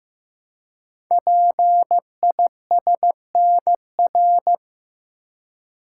Morse code, heard as a keyed tone, about 15 wpm, 700 Hz. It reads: PISNR
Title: Code Groups - 5 Letters - 15wpm